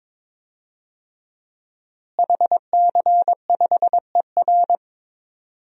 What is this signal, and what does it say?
Morse code, heard as a keyed tone, about 22 wpm, 700 Hz. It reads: HC5ER
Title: DXCC-100 Call Signs - 5 Characters - Repeated 2x - 22wpm